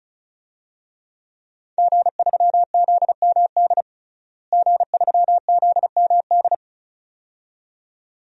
Morse code, heard as a keyed tone, about 36 wpm, 700 Hz. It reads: G3ZMD G3ZMD
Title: DXCC-100 Call Signs - 5 Characters - Repeated 2x - 35wpm